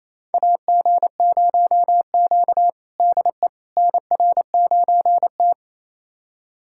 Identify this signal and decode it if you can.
Morse code, heard as a keyed tone, about 28 wpm, 700 Hz. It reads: AG0Q DE NR9T
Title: USA Callsign DE Callsign - 4 Characters - 28wpm